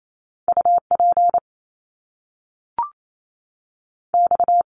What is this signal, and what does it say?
Morse code, heard as a keyed tone, about 28 wpm, 700 Hz. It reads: U P X
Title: Top Two Letter Combinations in Words 28wpm